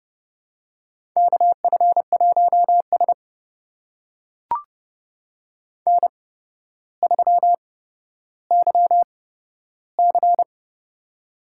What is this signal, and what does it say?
Morse code, heard as a keyed tone, about 30 wpm, spaced as for 13 wpm, 700 Hz. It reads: KF1S N3YC
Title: USA Call Signs - 4 Characters - 8x Character Spacing - 30wpm